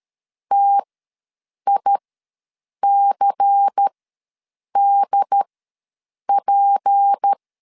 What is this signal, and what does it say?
Morse code, heard as a keyed tone, about 13 wpm, spaced as for 7 wpm, 800 Hz. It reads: TICDP
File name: LEZIONI / 6 PKX0 / 6 P ws 12wpm 3117